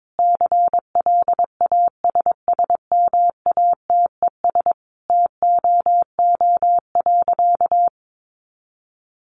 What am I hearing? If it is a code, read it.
Morse code, heard as a keyed tone, about 22 wpm, 700 Hz. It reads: CLASSMATES TOO.
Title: Additional Easy Phrases 22wpm